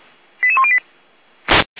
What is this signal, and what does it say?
Five-tone selective call (ZVEI), heard as quick runs of five tones, tones of about 70 ms. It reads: 88108